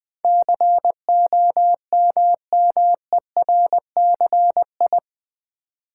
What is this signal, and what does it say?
Morse code, heard as a keyed tone, about 20 wpm, 700 Hz. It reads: COMMERCI
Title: Build A Word - 4 Letters At A Time - 10-Letter Words - 20wpm